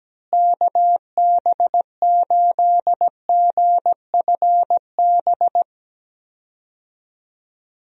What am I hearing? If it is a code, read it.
Morse code, heard as a keyed tone, about 17 wpm, 700 Hz. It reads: KB8GFB
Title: USA Call Signs - 6 Characters - Repeated 2x - 17wpm